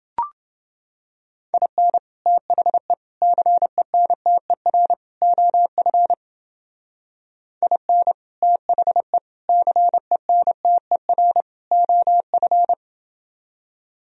Morse code, heard as a keyed tone, about 28 wpm, 700 Hz. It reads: IN THE CENTER OF IN THE CENTER OF